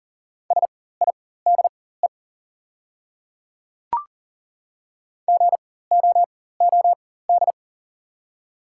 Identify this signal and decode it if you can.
Morse code, heard as a keyed tone, about 38 wpm, spaced as for 18 wpm, 700 Hz. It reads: SIDE GOOD
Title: 4x Character Spacing - Top 200 Words - 40wpm